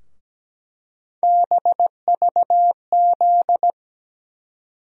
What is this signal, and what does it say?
Morse code, heard as a keyed tone, about 17 wpm, 700 Hz. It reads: BVZ
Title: USA Call Sign Postfixes - 3 Letters 17wpm